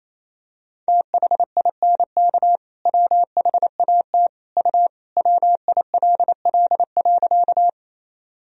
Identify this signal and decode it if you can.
Morse code, heard as a keyed tone, about 28 wpm, 700 Hz. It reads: THINK WHAT U WILL.
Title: Sentences from Top 200 Words and Abbreviations 28wpm